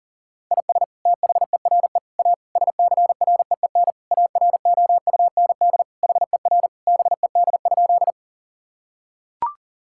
Morse code, heard as a keyed tone, about 40 wpm, 700 Hz. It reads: IS THERE A SCREEN AROUND HER BED?
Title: Sentences from Top 1000 Words - Review Part 6 - 40wpm